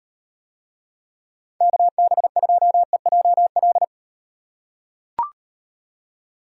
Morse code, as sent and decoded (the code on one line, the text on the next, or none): -.- -... ..--- . .--- .-..
KB2EJL